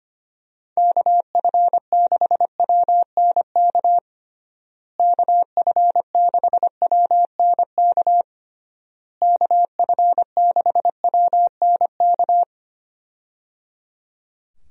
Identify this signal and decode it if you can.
Morse code, heard as a keyed tone, about 25 wpm, 700 Hz. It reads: KF6WNK KF6WNK KF6WNK
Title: USA Call Signs - 6 Characters - Repeated 3x - 25wpm